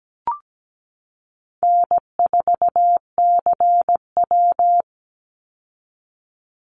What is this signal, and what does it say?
Morse code, heard as a keyed tone, about 17 wpm, 700 Hz. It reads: N4CW
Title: Easy Call Signs Repeated 2x - 17wpm